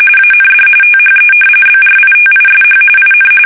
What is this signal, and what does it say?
RTTY, about 45.45 baud. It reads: AND THE BRITISH IN
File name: fsk850 / usnavy1